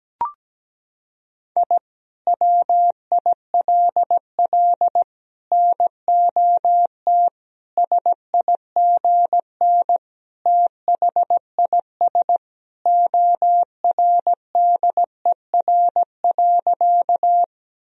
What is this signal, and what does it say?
Morse code, heard as a keyed tone, about 17 wpm, 700 Hz. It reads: I WILL NOT SIGN THIS ORDER.
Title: Sentences from Top 1000 Words - Review Part 2 - 17wpm